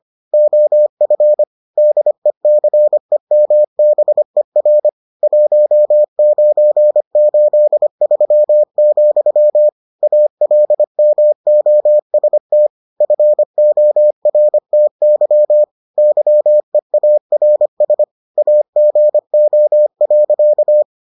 Morse code, heard as a keyed tone, about 25 wpm, 600 Hz. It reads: OF DECEMBER 1983, ALMOST FORTY YEARS AGO.